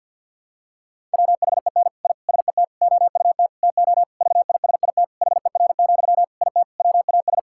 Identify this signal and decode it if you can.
Morse code, heard as a keyed tone, about 52 wpm, 700 Hz. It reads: WHEN I SET OUT TO VISIT HER, IT WAS
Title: Sentences from Top 1000 Words - Review Part 6 - 50wpm